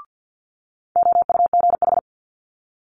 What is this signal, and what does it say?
Morse code, heard as a keyed tone, about 50 wpm, 700 Hz. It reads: OUGH
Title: Multi-letter Phonemes Quick Fire 50wpm